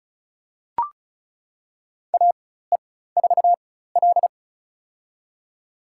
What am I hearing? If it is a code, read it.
Morse code, heard as a keyed tone, about 36 wpm, spaced as for 16 wpm, 700 Hz. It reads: AE4L
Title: USA Call Signs - 4x Character Spacing - 35wpm